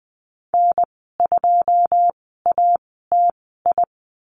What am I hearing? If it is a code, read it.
Morse code, heard as a keyed tone, about 20 wpm, spaced as for 15 wpm, 700 Hz. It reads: N2ATI
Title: 161 Lesson 26g - Letter g - Call Signs 20wpm